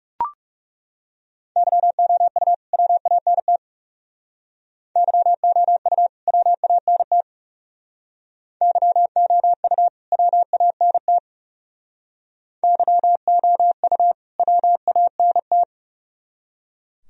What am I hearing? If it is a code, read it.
Morse code, heard as a keyed tone, about 37 wpm, 700 Hz. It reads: YOU WANT YOU WANT YOU WANT YOU WANT
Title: Speed-Racing - Sets of 2 Words 30wpm